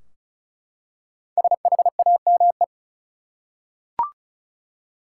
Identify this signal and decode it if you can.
Morse code, heard as a keyed tone, about 36 wpm, 700 Hz. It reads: SHAME
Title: Five Letter Words - 35wpm